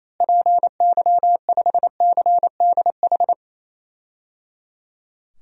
Morse code, heard as a keyed tone, about 28 wpm, 700 Hz. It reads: PY5CDH
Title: DXCC-100 Call Signs - 6 Characters - Repeated 3x - 28wpm